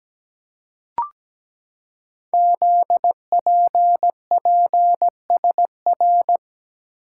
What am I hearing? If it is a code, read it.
Morse code, heard as a keyed tone, about 17 wpm, 700 Hz. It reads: ZPPSR
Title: Code Groups - 5 Letters - 17wpm